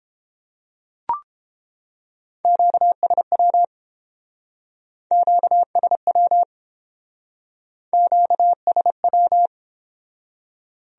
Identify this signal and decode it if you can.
Morse code, heard as a keyed tone, about 29 wpm, 700 Hz. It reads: QSW QSW QSW